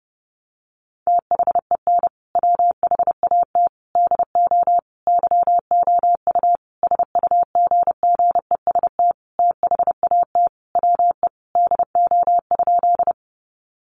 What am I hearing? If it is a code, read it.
Morse code, heard as a keyed tone, about 30 wpm, 700 Hz. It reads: THEN WHAT DO YOU SUGGEST THAT WE DO?
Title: Sentences from Top 1300 Words 30wpm